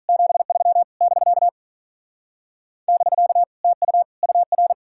Morse code, heard as a keyed tone, about 44 wpm, 700 Hz. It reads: 73 <BK> <BK> TU UR